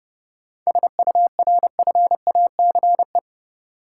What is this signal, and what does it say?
Morse code, heard as a keyed tone, about 31 wpm, 700 Hz. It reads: SURFACE
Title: Sentences from Top 600 Words - New Word Drill - 30wpm